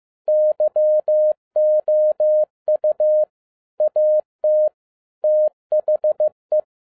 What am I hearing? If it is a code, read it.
Morse code, heard as a keyed tone, about 15 wpm, 610 Hz. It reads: YOU AT THE
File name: 2012-01-mp3 / n17-SUMMARY OF FEA NET IN 2011-15wpm